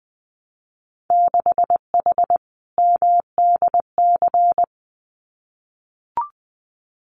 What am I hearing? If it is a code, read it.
Morse code, heard as a keyed tone, about 20 wpm, 700 Hz. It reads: 6H MDC
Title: Winter Field Day Exchanges Encore 20wpm